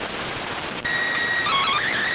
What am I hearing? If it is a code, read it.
SSTV (slow-scan television), PD 290.